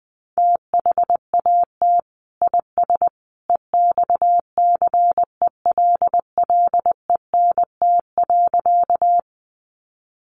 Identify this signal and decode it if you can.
Morse code, heard as a keyed tone, about 20 wpm, 700 Hz. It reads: THAT IS EXCELLENT.
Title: Sentences from Top 1000 Words 20wpm